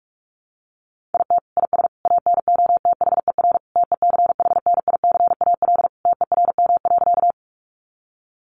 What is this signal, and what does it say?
Morse code, heard as a keyed tone, about 44 wpm, 700 Hz. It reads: IT IS ANOTHER TECHNICAL TERM.
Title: Sentences from Top 1100 Words 45wpm